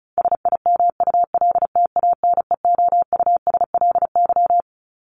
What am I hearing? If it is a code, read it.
Morse code, heard as a keyed tone, about 34 wpm, 700 Hz. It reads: SIMULTANEOUSLY
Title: LY Suffix - Top 100 Words with Base Words 35wpm